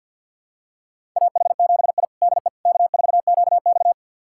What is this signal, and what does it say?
Morse code, heard as a keyed tone, about 46 wpm, 690 Hz. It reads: AH7I DE K4QX
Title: USA Callsign DE Callsign - 4 Characters - 50wpm